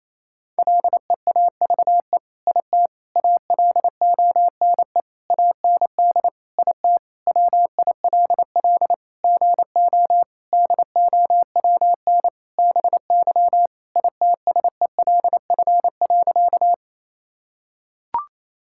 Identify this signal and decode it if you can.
Morse code, heard as a keyed tone, about 28 wpm, 700 Hz. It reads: LEAVE IT ALONE AND IT WILL GO DOWN BY ITSELF.
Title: Sentences from Top 1000 Words - Review Part 5 - 28wpm